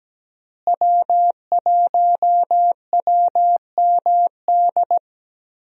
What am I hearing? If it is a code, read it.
Morse code, heard as a keyed tone, about 17 wpm, 700 Hz. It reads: W1WMD
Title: USA Call Signs - 5 Characters - Repeated 3x - 17wpm